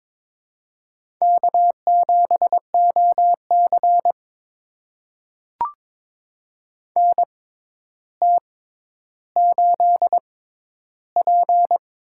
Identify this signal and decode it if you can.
Morse code, heard as a keyed tone, about 22 wpm, spaced as for 11 wpm, 700 Hz. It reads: K7OC NT8P
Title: USA Call Signs - 4 Characters - 6x Character Spacing - 22wpm